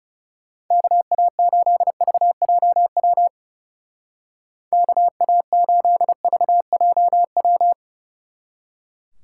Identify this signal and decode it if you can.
Morse code, heard as a keyed tone, about 37 wpm, 700 Hz. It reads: KA8VJW KA8VJW
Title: Speed Racing - USA Call Signs 30wpm